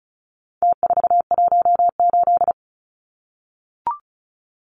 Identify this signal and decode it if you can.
Morse code, heard as a keyed tone, about 34 wpm, 700 Hz. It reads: T418
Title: Numbers - Contest Style - 4-Digit 35wpm